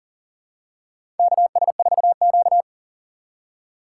Morse code, heard as a keyed tone, about 40 wpm, 700 Hz. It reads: KS4Q
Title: USA Call Signs - 4x Character Spacing - 40wpm